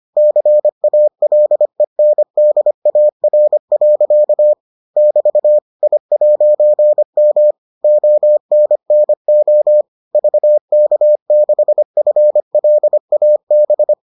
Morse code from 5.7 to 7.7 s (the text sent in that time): I'M